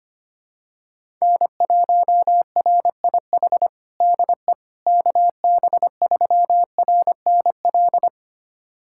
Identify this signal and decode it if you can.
Morse code, heard as a keyed tone, about 25 wpm, 700 Hz. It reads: N1RIH DE KB3RNL
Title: USA Callsign DE Callsign 25wpm